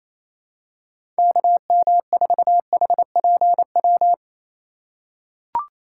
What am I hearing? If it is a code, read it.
Morse code, heard as a keyed tone, about 28 wpm, 700 Hz. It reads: KM4HPW
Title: USA Call Signs - 6 Characters - 4x Character Spacing - 28wpm